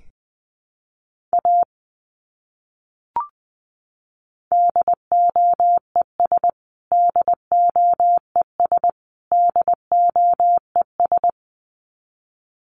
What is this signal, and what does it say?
Morse code, heard as a keyed tone, about 20 wpm, 700 Hz. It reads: A DOES DOES DOES
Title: Top 200 Words Repeated 20wpm